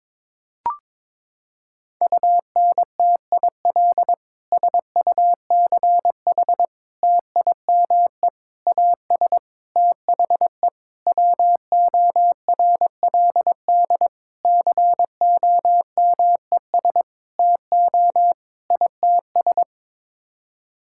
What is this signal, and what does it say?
Morse code, heard as a keyed tone, about 22 wpm, 700 Hz. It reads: UNTIL SUCH TIME AS THE WORLD COMES TO ITS
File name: Sets of 9 Words - 22wpm